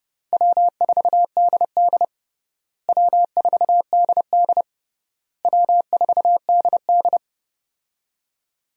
Morse code, heard as a keyed tone, about 30 wpm, 700 Hz. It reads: W4DD W4DD W4DD
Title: USA Call Signs - 4 Characters - Repeated 3x - 30wpm